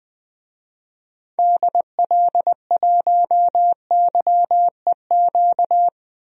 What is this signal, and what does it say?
Morse code, heard as a keyed tone, about 20 wpm, 700 Hz. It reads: DL1YEQ
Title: DXCC-100 Call Signs - 6 Characters - 20wpm